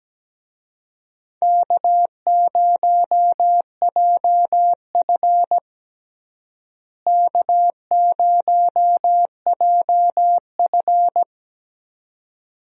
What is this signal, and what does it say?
Morse code, heard as a keyed tone, about 17 wpm, 700 Hz. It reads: K0JF K0JF
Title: USA Call Signs - 4 Characters - Repeated 2x - 17wpm